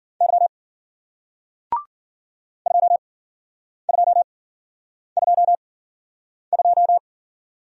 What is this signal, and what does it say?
Morse code, heard as a keyed tone, about 55 wpm, spaced as for 9 wpm, 700 Hz. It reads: Y 2222